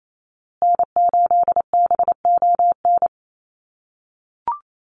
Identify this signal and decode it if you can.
Morse code, heard as a keyed tone, about 28 wpm, 700 Hz. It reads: N8BON
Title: USA Call Signs Repeated 2x 28wpm